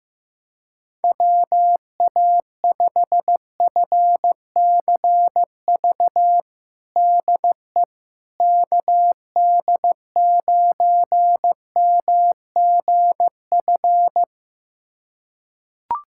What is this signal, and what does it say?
Morse code, heard as a keyed tone, about 15 wpm, 700 Hz. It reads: WA5FCV DE KD9MGF
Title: USA Callsign DE Callsign - 6 Characters - 15wpm